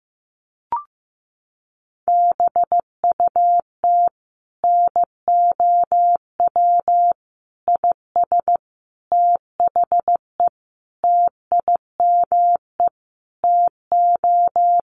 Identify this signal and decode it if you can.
Morse code, heard as a keyed tone, about 14 wpm, 700 Hz. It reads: BUT NOW IS THE TIME TO
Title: Sentences from Top 1500 Words 15wpm